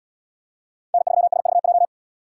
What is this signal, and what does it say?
Morse code, heard as a keyed tone, about 70 wpm, 700 Hz. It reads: N3SVJ